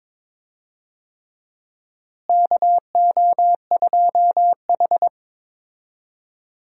Morse code, heard as a keyed tone, about 22 wpm, 700 Hz. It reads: KO2H
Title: USA Call Signs - 4 Characters - Repeated 3x - 22wpm